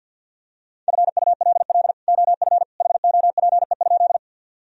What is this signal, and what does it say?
Morse code, heard as a keyed tone, about 50 wpm, 700 Hz. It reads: FULL OF HOPE?
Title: Sentences from Top 1000 Words - Review Part 3 - 50wpm